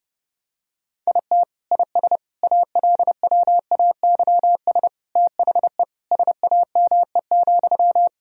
Morse code, heard as a keyed tone, about 30 wpm, 700 Hz. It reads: IT IS ALWAYS THE SAME,
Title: Sentences from Top 1000 Words - Review Part 1 - 30wpm